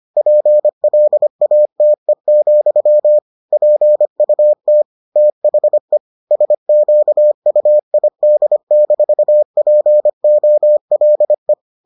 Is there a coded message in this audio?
Yes: Morse code from 0.2 to 11.5 s, heard as a keyed tone, about 25 wpm, 600 Hz. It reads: PLATE, PUT THE SQUID-POLE